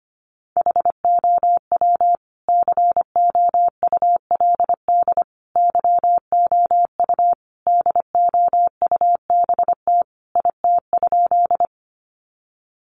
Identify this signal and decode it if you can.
Morse code, heard as a keyed tone, about 25 wpm, 700 Hz. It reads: HOW COULD YOU DOUBT IT?
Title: Sentences from Top 1500 Words 25wpm